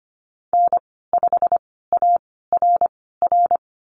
Morse code, heard as a keyed tone, about 25 wpm, spaced as for 16 wpm, 700 Hz. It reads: N5ARR
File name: Lesson 15g - Number 5 - Call Signs 25wpm